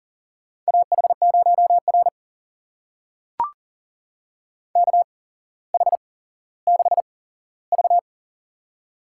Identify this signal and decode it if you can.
Morse code, heard as a keyed tone, about 40 wpm, spaced as for 17 wpm, 700 Hz. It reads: AH0R KH6V
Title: USA Call Signs - 8x Character Spacing - 40wpm